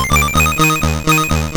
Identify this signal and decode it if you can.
There is no code.